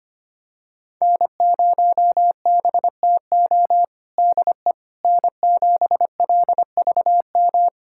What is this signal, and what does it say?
Morse code, heard as a keyed tone, about 25 wpm, 700 Hz. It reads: N0BTO DE N7LVM
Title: USA Callsign DE Callsign - 5 Characters - 25wpm